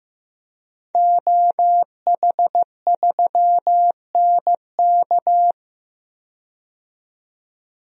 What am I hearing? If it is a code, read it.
Morse code, heard as a keyed tone, about 15 wpm, 700 Hz. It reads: OH3NK
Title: DXCC-100 Call Signs - 5 Characters - 15wpm